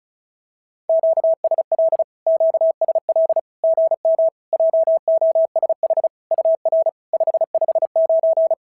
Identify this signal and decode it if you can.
Morse code, heard as a keyed tone, about 36 wpm, 690 Hz. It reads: QSL QSL GM JOSH UR 559
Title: SOTA QSO Practice 35wpm